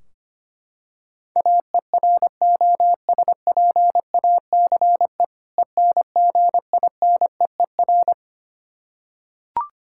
Morse code, heard as a keyed tone, about 25 wpm, 700 Hz. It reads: AEROSPACE ENGINEER
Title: Practice QSO - Exchange 3 - More Info - 25wpm